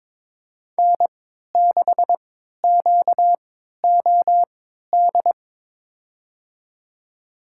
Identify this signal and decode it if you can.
Morse code, heard as a keyed tone, about 22 wpm, spaced as for 12 wpm, 700 Hz. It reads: N6QOD
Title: USA Call Signs - 3x Character Spacing - 22wpm